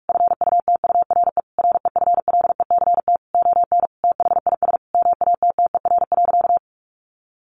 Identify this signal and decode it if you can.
Morse code, heard as a keyed tone, about 44 wpm, 700 Hz. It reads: FUTURE REFLECT ON THIS MATTER.
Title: Sentences from Top 1500 Words 45wpm